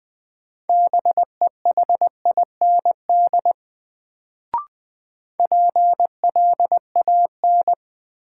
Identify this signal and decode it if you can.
Morse code, heard as a keyed tone, about 20 wpm, 700 Hz. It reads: BEHIND PLAN
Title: Sentences from Top 500 Words - New Word Drill - 20wpm